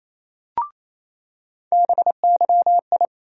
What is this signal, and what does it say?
Morse code, heard as a keyed tone, about 28 wpm, 700 Hz. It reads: BYI
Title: USA Call Sign Postfixes - 3 Letters 28wpm